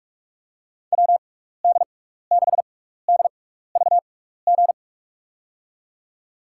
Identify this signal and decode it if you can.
Morse code, heard as a keyed tone, about 47 wpm, spaced as for 16 wpm, 700 Hz. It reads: WD6DVG
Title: USA Call Signs - 6 Characters - 6x Character Spacing - 45wpm